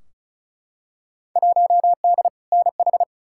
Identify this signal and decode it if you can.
Morse code, heard as a keyed tone, about 36 wpm, 700 Hz. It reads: 1D NH
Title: Field Day Exchanges - 35wpm